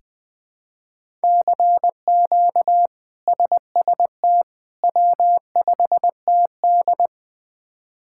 Morse code, heard as a keyed tone, about 20 wpm, 700 Hz. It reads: CQ SST W5TD
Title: K1USN SST QSO Practice - 20wpm